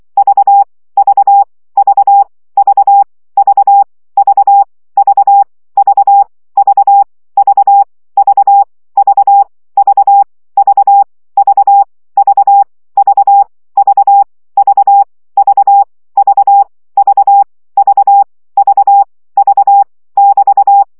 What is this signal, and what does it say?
Morse code, heard as a keyed tone, about 24 wpm, 800 Hz. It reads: V V V V V V V V V V V V V V V V V V V V V V V V V <BT>
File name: CW gyakorlas v 20 baud